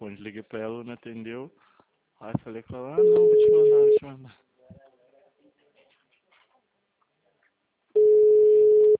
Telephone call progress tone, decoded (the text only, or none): ringback tone